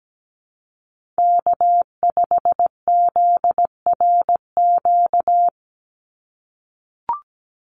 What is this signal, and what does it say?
Morse code, heard as a keyed tone, about 17 wpm, 700 Hz. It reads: K5ZRQ
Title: USA Call Signs 17wpm